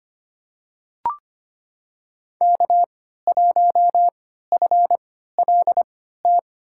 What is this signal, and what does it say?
Morse code, heard as a keyed tone, about 25 wpm, spaced as for 14 wpm, 700 Hz. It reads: K1FLT
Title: USA Call Signs - 5 Characters - 3x Character Spacing - 25wpm